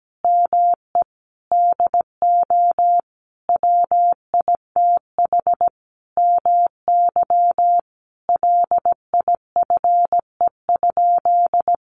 Morse code, heard as a keyed tone, about 17 wpm, 700 Hz. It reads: ME DO WITH MY LIFE?
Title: Sentences from Top 300 Words 17wpm